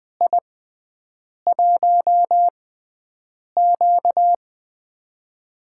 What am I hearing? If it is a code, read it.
Morse code, heard as a keyed tone, about 20 wpm, spaced as for 7 wpm, 700 Hz. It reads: I1Q
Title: USA Call Signs - 6x Character Spacing - 20wpm